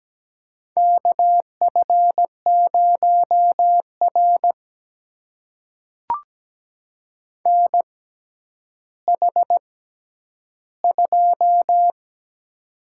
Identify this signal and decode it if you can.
Morse code, heard as a keyed tone, about 17 wpm, spaced as for 10 wpm, 700 Hz. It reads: KF0R NH2
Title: USA Call Signs - 4 Characters - 6x Character Spacing - 17wpm